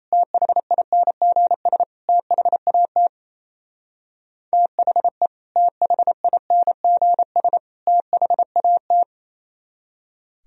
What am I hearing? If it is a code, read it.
Morse code, heard as a keyed tone, about 35 wpm, 700 Hz. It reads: THINGS THAT THE THINGS THAT
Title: Speed-Racing - Sets of 3 Words 28wpm